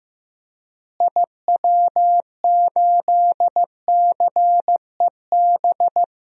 Morse code, heard as a keyed tone, about 16 wpm, 700 Hz. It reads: IW8CEB
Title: DXCC-100 Call Signs - 6 Characters - Repeated 2x - 15wpm